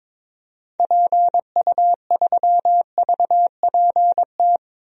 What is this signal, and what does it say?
Morse code, heard as a keyed tone, about 22 wpm, 700 Hz. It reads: PU3VPT
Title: DXCC-100 Call Signs - 6 Characters - 22wpm